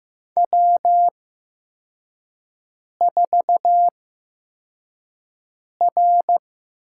Morse code, heard as a keyed tone, about 15 wpm, spaced as for 4 wpm, 700 Hz. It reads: W4R
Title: USA Call Signs - 5 Characters - 8x Character Spacing - 15wpm